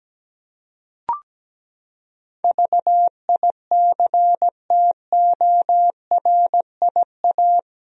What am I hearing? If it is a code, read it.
Morse code, heard as a keyed tone, about 17 wpm, 700 Hz. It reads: VICTORIA
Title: USA Common Names - 17wpm